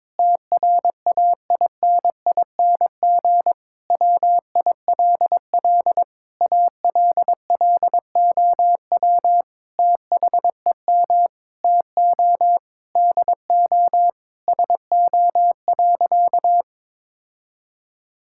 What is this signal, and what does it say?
Morse code, heard as a keyed tone, about 22 wpm, 700 Hz. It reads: TRAINING WILL ALLOW THEM TO DO SO.
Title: Sentences from Top 1000 Words - Review Part 5 - 22wpm